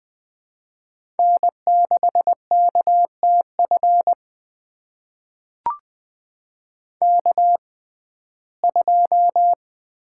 Morse code, heard as a keyed tone, about 20 wpm, spaced as for 15 wpm, 700 Hz. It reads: N6KTF K2
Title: USA Call Signs - 5 Characters - 6x Character Spacing - 20wpm